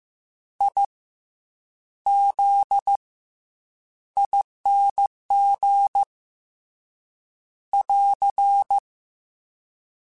Morse code, heard as a keyed tone, about 15 wpm, 800 Hz. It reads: I Z ING <AR>
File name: Special INGs / Special INGs-mp3s / SlowToFaster ing @13